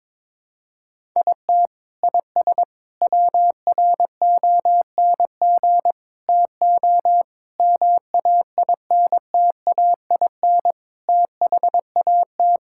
Morse code, heard as a keyed tone, about 22 wpm, 700 Hz. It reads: IT IS WRONG TO MAINTAIN THAT
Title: Sentences from Top 1000 Words - Review Part 5 - 22wpm